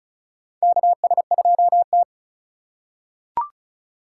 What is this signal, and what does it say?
Morse code, heard as a keyed tone, about 34 wpm, 700 Hz. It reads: KS2T
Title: USA Call Signs - 4 Characters - 2x Character Spacing - 35wpm